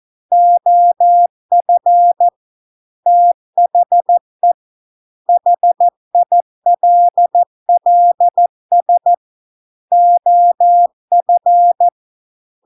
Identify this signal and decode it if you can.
Morse code, heard as a keyed tone, about 14 wpm, 700 Hz. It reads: OF THE HILLS OF